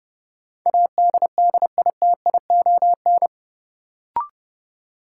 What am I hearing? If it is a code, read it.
Morse code, heard as a keyed tone, about 30 wpm, 700 Hz. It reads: ADDITION
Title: Sentences from Top 1500 Words - New Word Drill Alt - 30wpm